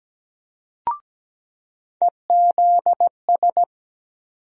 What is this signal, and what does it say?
Morse code, heard as a keyed tone, about 17 wpm, 700 Hz. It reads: EZS
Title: USA Call Sign Postfixes - 3 Letters 17wpm